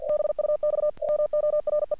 Morse code, heard as a keyed tone, about 48 wpm, 600 Hz. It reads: BUX MORE
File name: archived / CRSnet